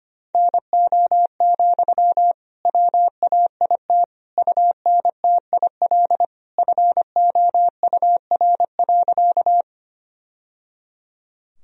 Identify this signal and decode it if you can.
Morse code, heard as a keyed tone, about 25 wpm, 700 Hz. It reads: NO, WAIT UNTIL FOUR.
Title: Sentences from Top 1000 Words - Review Part 1 - 25wpm